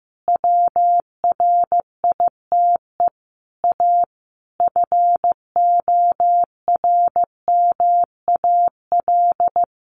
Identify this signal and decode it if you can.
Morse code, heard as a keyed tone, about 15 wpm, 700 Hz. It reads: WRITE A FORMAL
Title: Sentences from Top 1300 Words 15wpm